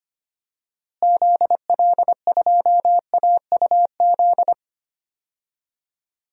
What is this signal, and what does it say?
Morse code, heard as a keyed tone, about 25 wpm, 700 Hz. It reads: ZL2AUZ
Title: DXCC-100 Call Signs - 6 Characters - Repeated 3x - 25wpm